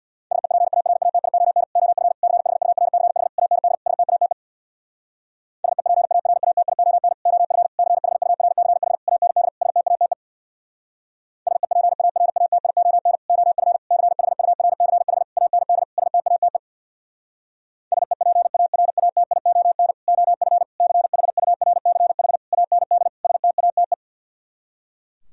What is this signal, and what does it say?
Morse code, heard as a keyed tone, about 60 wpm, 700 Hz. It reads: SEPARATION OF CHURCH AND STATE SEPARATION OF CHURCH AND STATE SEPARATION OF CHURCH AND STATE SEPARATION OF CHURCH AND STATE
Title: Speed-Racing - Sets of 5 Words 50wpm